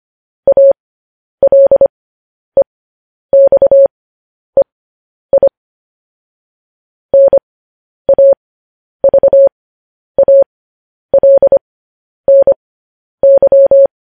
Morse code, heard as a keyed tone, about 25 wpm, spaced as for 10 wpm, 550 Hz. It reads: ALEXEI NAVALNY